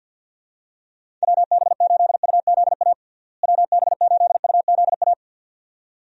Morse code, heard as a keyed tone, about 50 wpm, 700 Hz. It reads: WB8UZA WB8UZA
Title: USA Call Signs - 6 Characters - Repeated 2x - 50wpm